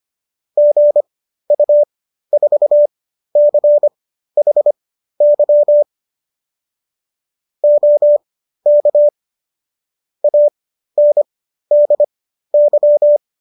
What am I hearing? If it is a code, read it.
Morse code, heard as a keyed tone, about 25 wpm, spaced as for 13 wpm, 590 Hz. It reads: GU4CHY OK ANDY